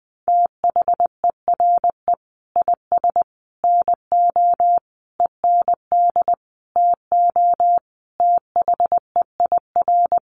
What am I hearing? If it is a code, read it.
Morse code, heard as a keyed tone, about 20 wpm, 700 Hz. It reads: THERE IS NO END TO THEIR
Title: Sentences from Top 1100 Words 20wpm